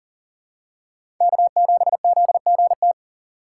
Morse code, heard as a keyed tone, about 40 wpm, 700 Hz. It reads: K7ZGT
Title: USA Call Signs - 2x Character Spacing - 40wpm